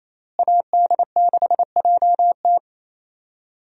Morse code, heard as a keyed tone, about 28 wpm, 700 Hz. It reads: AD6JT
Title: USA Call Signs - 5 Characters - Repeated 2x - 28wpm